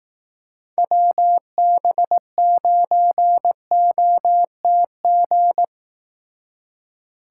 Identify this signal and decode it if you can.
Morse code, heard as a keyed tone, about 18 wpm, 700 Hz. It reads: WB9OTG
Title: Speed Racing - USA 2x3 Call Signs 15wpm